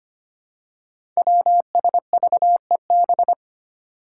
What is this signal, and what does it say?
Morse code, heard as a keyed tone, about 25 wpm, 700 Hz. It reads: WSVEB
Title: Five Challenging Letter-Number Combinations 25wpm